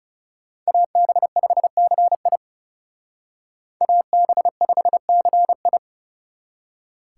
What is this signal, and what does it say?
Morse code, heard as a keyed tone, about 37 wpm, 700 Hz. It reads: AB5CI AB5CI
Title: Speed Racing - USA 2x2 Call Signs 30wpm